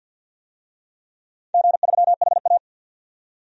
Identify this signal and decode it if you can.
Morse code, heard as a keyed tone, about 50 wpm, 700 Hz. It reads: G3HA
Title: DXCC-100 Call Signs - 4 Characters - 50wpm